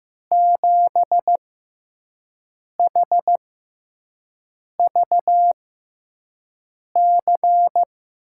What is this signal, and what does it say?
Morse code, heard as a keyed tone, about 15 wpm, spaced as for 5 wpm, 700 Hz. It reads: 7HVC
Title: USA Call Signs - 5 Characters - 6x Character Spacing - 15wpm